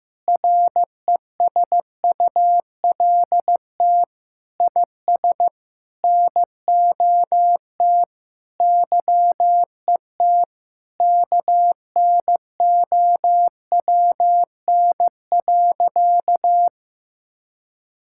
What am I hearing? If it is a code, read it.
Morse code, heard as a keyed tone, about 15 wpm, 700 Hz. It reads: RESULT IS NOT YET KNOWN.
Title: Sentences from Top 1000 Words - Review Part 5 - 15wpm